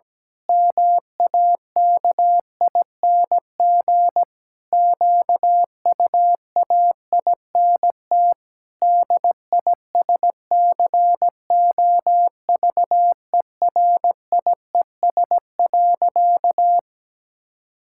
Morse code, heard as a keyed tone, about 17 wpm, 700 Hz. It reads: MAKING QUAINT DISCOVERIES.
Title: Pangram Challenge v2 - 17wpm